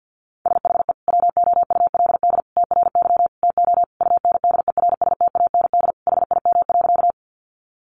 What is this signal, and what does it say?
Morse code, heard as a keyed tone, about 50 wpm, 700 Hz. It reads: SHE WOULD TRY TO UNDERSTAND HIM.
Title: Sentences from Top 600 Words 50wpm